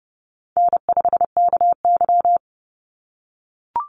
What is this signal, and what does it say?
Morse code, heard as a keyed tone, about 30 wpm, 700 Hz. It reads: N5KY
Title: Easy Call Signs - 30wpm